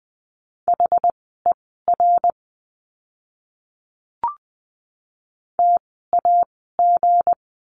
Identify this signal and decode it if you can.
Morse code, heard as a keyed tone, about 20 wpm, spaced as for 14 wpm, 700 Hz. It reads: HER TAG
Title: 160 Lesson 26f - Letter g - Words 20wpm